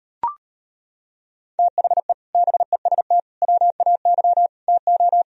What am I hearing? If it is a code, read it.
Morse code, heard as a keyed tone, about 30 wpm, 700 Hz. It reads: THE BEST WAY TO
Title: Speed-Racing - Sets of 4 Words 25wpm